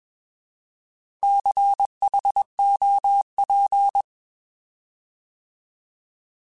Morse code, heard as a keyed tone, about 21 wpm, 800 Hz. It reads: CHOP